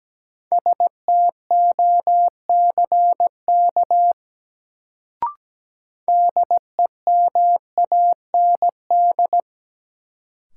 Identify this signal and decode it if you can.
Morse code, heard as a keyed tone, about 17 wpm, 700 Hz. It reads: STOCK DEMAND
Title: Sentences from Top 800 Words - New Word Drill - 17wpm